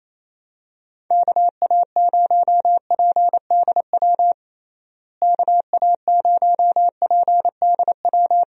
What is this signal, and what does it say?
Morse code, heard as a keyed tone, about 27 wpm, 700 Hz. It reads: KA0PDW KA0PDW
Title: USA Call Signs - 6 Characters - Repeated 3x - 28wpm